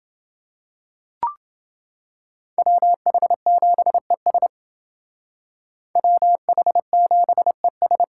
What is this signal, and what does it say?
Morse code, heard as a keyed tone, about 28 wpm, 700 Hz. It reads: WH7ES WH7ES